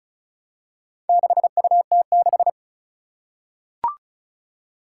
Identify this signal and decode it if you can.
Morse code, heard as a keyed tone, about 36 wpm, 700 Hz. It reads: 6UT6
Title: Four Challenging Letter-Number Combinations 35wpm